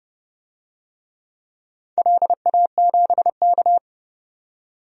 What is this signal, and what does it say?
Morse code, heard as a keyed tone, about 32 wpm, 700 Hz. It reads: LA7K
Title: DXCC-100 Call Signs - 4 Characters - 30wpm